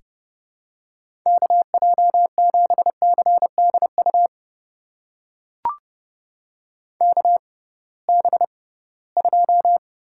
Morse code, heard as a keyed tone, about 30 wpm, spaced as for 19 wpm, 700 Hz. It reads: KJ7CDU KB2
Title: USA Call Signs - 6 Characters - 6x Character Spacing - 30wpm